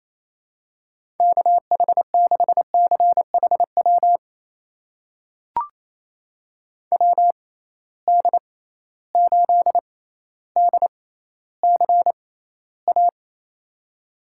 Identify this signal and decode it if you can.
Morse code, heard as a keyed tone, about 28 wpm, spaced as for 14 wpm, 700 Hz. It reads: KH6CHW WD8DCA
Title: USA Call Signs - 6 Characters - 6x Character Spacing - 28wpm